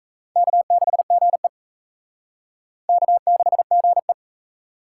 Morse code, heard as a keyed tone, about 35 wpm, 700 Hz. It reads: K6GE K6GE